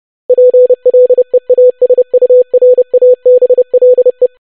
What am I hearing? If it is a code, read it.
Morse code, heard as a keyed tone, about 30 wpm, 500 Hz. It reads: PLEASURABLE
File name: longwords-25wpm